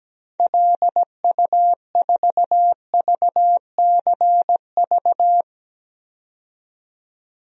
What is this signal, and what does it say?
Morse code, heard as a keyed tone, about 17 wpm, 700 Hz. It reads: LU4VCV
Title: DXCC-100 Call Signs - 6 Characters - 17wpm